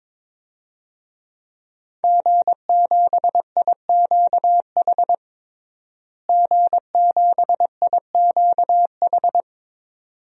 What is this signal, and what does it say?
Morse code, heard as a keyed tone, about 22 wpm, 700 Hz. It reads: G7IQH G7IQH
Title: DXCC-100 Call Signs - 5 Characters - Repeated 2x - 22wpm